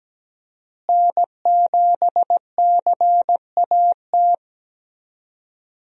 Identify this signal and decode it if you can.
Morse code, heard as a keyed tone, about 17 wpm, 700 Hz. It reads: N7CAT
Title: USA Call Signs - 5 Characters - Repeated 3x - 17wpm